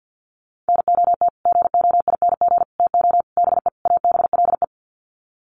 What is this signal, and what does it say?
Morse code, heard as a keyed tone, about 48 wpm, 700 Hz. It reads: NOT GOING TO BE ABLE
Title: Sets of 5 Words 50wpm